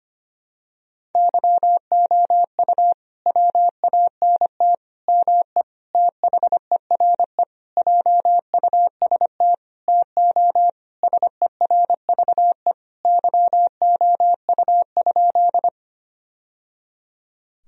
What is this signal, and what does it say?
Morse code, heard as a keyed tone, about 25 wpm, 700 Hz. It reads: YOU WANT ME THERE JUST TO SERVE YOU?
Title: Sentences from Top 1000 Words - Review Part 2 - 25wpm